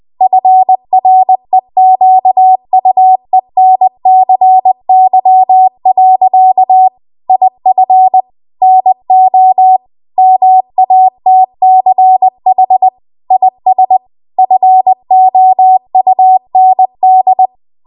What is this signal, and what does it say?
Morse code, heard as a keyed tone, about 20 wpm, 750 Hz. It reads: FREQUENCY. IF NO MATCH IS FOUND
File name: Archive / 20 WPM / 20WPM